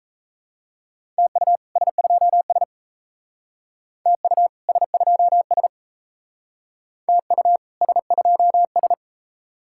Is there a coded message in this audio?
Yes: Morse code from 1.2 to 8.9 s, heard as a keyed tone, about 39 wpm, 700 Hz. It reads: TU S2S TU S2S TU S2S